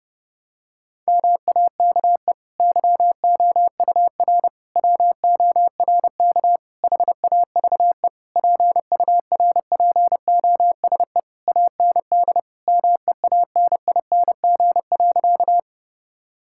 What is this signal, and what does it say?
Morse code, heard as a keyed tone, about 30 wpm, 700 Hz. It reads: MAKE YOUR WORK HAVE PURPOSE AND MEANING.
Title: Sentences from Top 1000 Words - Review Part 6 - 30wpm